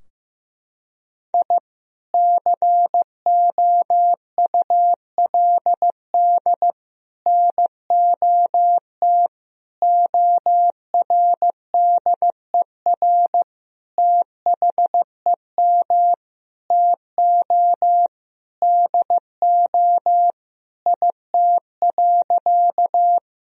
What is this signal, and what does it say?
Morse code, heard as a keyed tone, about 15 wpm, 700 Hz. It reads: I COULD NOT ORDER THEM TO DO IT.
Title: Sentences from Top 1000 Words - Review Part 2 - 15wpm